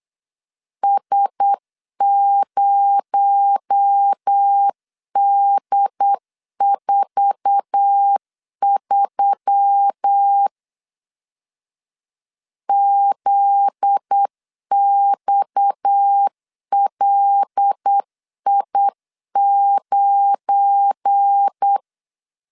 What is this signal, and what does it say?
Morse code, heard as a keyed tone, about 8 wpm, 800 Hz. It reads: S0D43 ZXLI9
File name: LEZIONI / 9 28 8wpm 3512